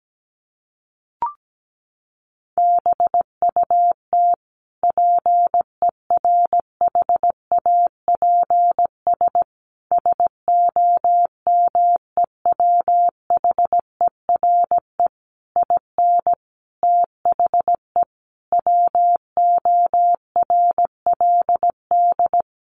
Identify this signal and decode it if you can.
Morse code, heard as a keyed tone, about 17 wpm, 700 Hz. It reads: BUT PERHAPS SOMEWHERE IN THE WORLD
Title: Sentences from Top 1400 Words 17wpm